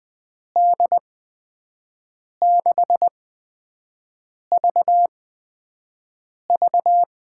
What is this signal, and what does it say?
Morse code, heard as a keyed tone, about 20 wpm, spaced as for 5 wpm, 700 Hz. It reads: D6VV
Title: USA Call Signs - 8x Character Spacing - 20wpm